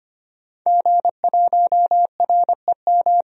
Morse code, heard as a keyed tone, about 25 wpm, 700 Hz. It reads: G1REM
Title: DXCC-100 Call Signs - 5 Characters - 25wpm